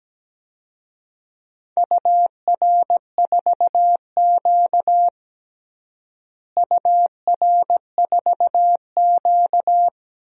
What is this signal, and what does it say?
Morse code, heard as a keyed tone, about 17 wpm, 700 Hz. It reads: UR4Q UR4Q
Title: DXCC-100 Call Signs - 4 Characters - Repeated 2x - 17wpm